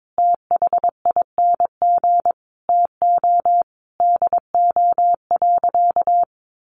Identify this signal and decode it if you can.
Morse code, heard as a keyed tone, about 22 wpm, 700 Hz. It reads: THING TO DO.
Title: Sentences from Top 100 Words 22wpm